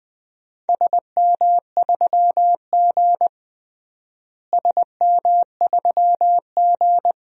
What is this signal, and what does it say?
Morse code, heard as a keyed tone, about 21 wpm, 700 Hz. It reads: SM3G SM3G
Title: DXCC-100 Call Signs - 4 Characters - Repeated 2x - 20wpm